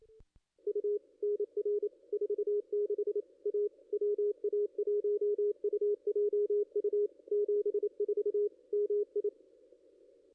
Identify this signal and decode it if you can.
Morse code, heard as a keyed tone, about 27 wpm, 410 Hz, with radio noise. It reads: U NR 46 A WA1UJU 74 MI
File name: contests / ss-cw-k5tr-op-n5rz -2200